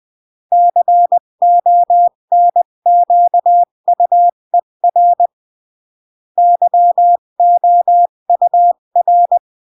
Morse code, heard as a keyed tone, about 20 wpm, 700 Hz. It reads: CONQUER YOUR
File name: CW / jodorowsky / 82 commandments 20-15